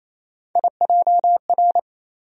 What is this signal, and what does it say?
Morse code, heard as a keyed tone, about 28 wpm, 700 Hz. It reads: IJR
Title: USA Call Sign Postfixes - 3 Letters 28wpm